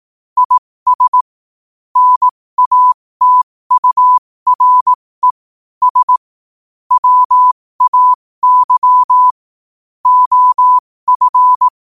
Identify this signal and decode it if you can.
Morse code, heard as a keyed tone, about 18 wpm, spaced as for 15 wpm, 1000 Hz. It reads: IS NATURE S WAY OF
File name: qotdfiles / 15wpm